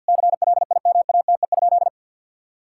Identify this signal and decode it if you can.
Morse code, heard as a keyed tone, about 50 wpm, 700 Hz. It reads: CLIMATE?
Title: Sentences from Top 1600 Words 50wpm